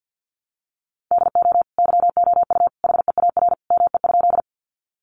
Morse code, heard as a keyed tone, about 50 wpm, 700 Hz. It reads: DO YOU HEAR ME?
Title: Sentences from Top 300 Words 50wpm